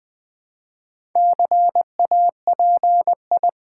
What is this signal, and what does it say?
Morse code, heard as a keyed tone, about 20 wpm, 700 Hz. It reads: CAPI
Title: Build A Word - 4 Letters At A Time - 10-Letter Words - 20wpm